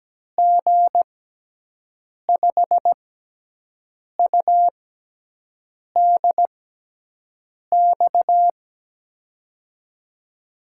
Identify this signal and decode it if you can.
Morse code, heard as a keyed tone, about 17 wpm, spaced as for 6 wpm, 700 Hz. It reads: G5UDX
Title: USA Call Signs - 6 Characters - 6x Character Spacing - 17wpm